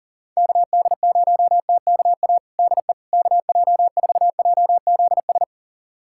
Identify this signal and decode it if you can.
Morse code, heard as a keyed tone, about 40 wpm, 700 Hz. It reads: KD0TKA DE KJ4JZS
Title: USA Callsign DE Callsign - 6 Characters - 40wpm